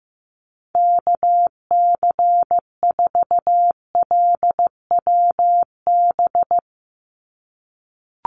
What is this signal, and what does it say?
Morse code, heard as a keyed tone, about 15 wpm, 700 Hz. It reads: KC4LWB
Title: USA Call Signs 15wpm